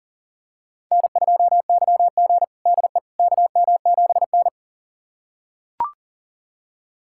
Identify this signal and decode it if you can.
Morse code, heard as a keyed tone, about 40 wpm, 700 Hz. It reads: N2YG DE KM7N
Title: USA Callsign DE Callsign - 4 Characters - 40wpm